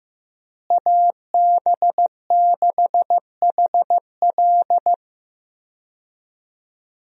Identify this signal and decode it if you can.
Morse code, heard as a keyed tone, about 15 wpm, 700 Hz. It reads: AB6HL
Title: USA Call Signs - 5 Characters - Repeated 3x - 15wpm